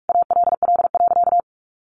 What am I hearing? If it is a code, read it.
Morse code, heard as a keyed tone, about 44 wpm, 700 Hz. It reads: ALL.
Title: Sentences from Top 1500 Words 45wpm